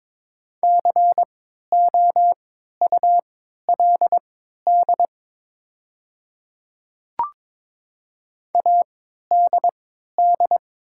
Morse code, heard as a keyed tone, about 22 wpm, spaced as for 12 wpm, 700 Hz. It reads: COULD ADD
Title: 3x Character Spacing - Top 300 Words - 22wpm